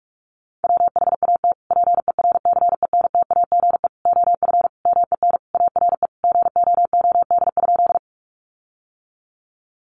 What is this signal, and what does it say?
Morse code, heard as a keyed tone, about 44 wpm, 700 Hz. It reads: WHAT PERCENTAGE OF MEN ARE GOOD?
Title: Sentences from Top 1300 Words 45wpm